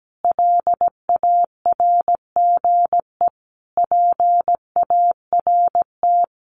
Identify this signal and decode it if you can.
Morse code, heard as a keyed tone, about 17 wpm, 700 Hz. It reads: LARGE PART
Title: Sentences from Top 1400 Words 17wpm